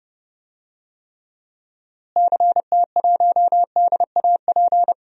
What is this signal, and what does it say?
Morse code, heard as a keyed tone, about 30 wpm, 700 Hz. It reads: CT1DAP
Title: DXCC-100 Call Signs - 6 Characters - Repeated 3x - 30wpm